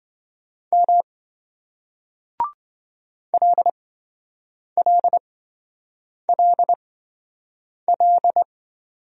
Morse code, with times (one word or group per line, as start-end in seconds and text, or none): 0.7-1.0 M
3.3-8.4 LLLL